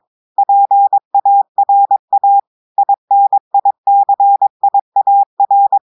Morse code, heard as a keyed tone, about 22 wpm, 800 Hz. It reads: PARA INICIAR
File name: audios / portugais / portugais texte en clair 2 vitesse 22 mots minutes